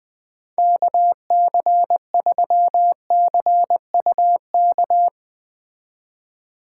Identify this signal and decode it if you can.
Morse code, heard as a keyed tone, about 20 wpm, 700 Hz. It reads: KC3CUK